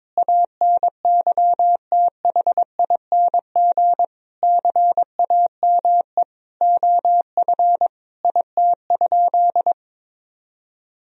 Morse code, heard as a keyed tone, about 21 wpm, 700 Hz. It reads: ANYTHING CAME OF IT?
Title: Sentences from Top 1000 Words - Review Part 3 - 22wpm